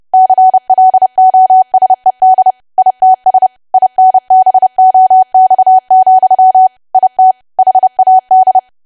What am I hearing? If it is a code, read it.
Morse code, heard as a keyed tone, about 30 wpm, 750 Hz. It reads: CLOSED ITS INBOX, IT HAD
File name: cwa / W1AW-Level3 / 30WPM